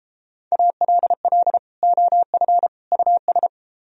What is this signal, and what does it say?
Morse code, heard as a keyed tone, about 30 wpm, 700 Hz. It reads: ALL OF US